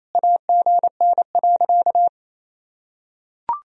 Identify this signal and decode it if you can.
Morse code, heard as a keyed tone, about 28 wpm, 700 Hz. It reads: AGN.
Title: Sentences from Top 500 Words and Abbreviations 28wpm